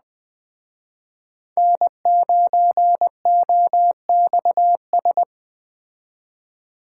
Morse code, heard as a keyed tone, about 20 wpm, 700 Hz. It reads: N9OXS
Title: Speed Racing - USA 1x3 Call Signs 17wpm